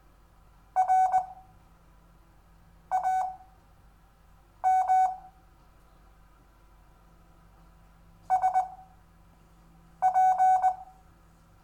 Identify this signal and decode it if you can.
Morse code, heard as a keyed tone, about 20 wpm, spaced as for 5 wpm, 750 Hz. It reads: RAM SP
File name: cw / Lesson4 REVIEW